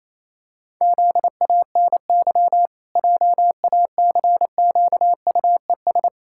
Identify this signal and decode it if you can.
Morse code, heard as a keyed tone, about 28 wpm, 700 Hz. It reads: ZANY JACQUES
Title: Pangram Challenge v2 - 28wpm